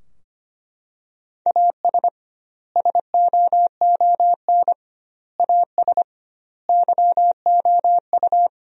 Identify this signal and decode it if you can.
Morse code, heard as a keyed tone, about 25 wpm, 700 Hz. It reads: AS SOON AS YOU
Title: 2x Word-Spacing - Sentences from Top 400 Words - 25wpm